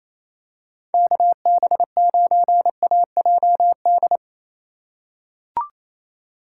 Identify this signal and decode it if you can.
Morse code, heard as a keyed tone, about 28 wpm, 700 Hz. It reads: KB9AJD
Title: USA Call Signs - 3x Character Spacing - 28wpm